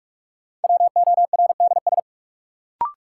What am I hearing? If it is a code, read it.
Morse code, heard as a keyed tone, about 46 wpm, 700 Hz. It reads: WORDS
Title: Sentences from Top 1000 Words - Review Part 6 - New Word Drill - 45wpm